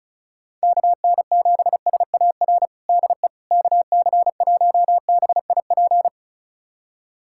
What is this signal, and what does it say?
Morse code, heard as a keyed tone, about 36 wpm, 700 Hz. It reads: KN7SAR DE KC1BIP
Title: USA Callsign DE Callsign 35wpm